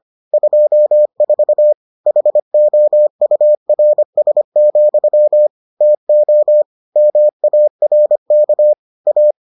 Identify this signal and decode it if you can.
Morse code, heard as a keyed tone, about 25 wpm, 600 Hz. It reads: 24 HOURS, TO MARK A